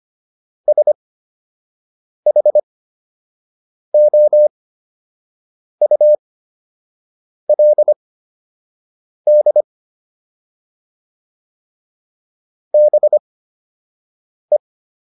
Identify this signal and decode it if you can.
Morse code, heard as a keyed tone, about 25 wpm, spaced as for 6 wpm, 600 Hz. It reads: SHOULD BE